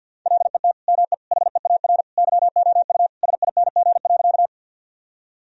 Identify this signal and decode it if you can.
Morse code, heard as a keyed tone, about 50 wpm, 700 Hz. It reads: LET ME HEAR YOU SING.
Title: Sentences from Top 1000 Words - Review Part 2 - 50wpm